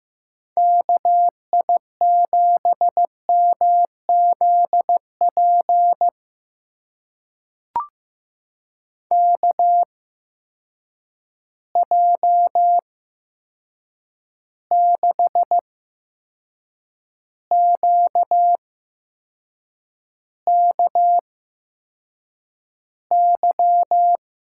Morse code, heard as a keyed tone, about 15 wpm, spaced as for 6 wpm, 700 Hz. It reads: KI7MZP KJ6QKY
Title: USA Call Signs - 8x Character Spacing - 15wpm